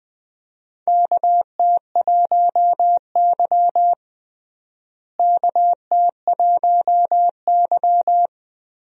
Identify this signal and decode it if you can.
Morse code, heard as a keyed tone, about 20 wpm, 700 Hz. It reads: KT1Y KT1Y
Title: USA Call Signs - 4 Characters - Repeated 2x - 20wpm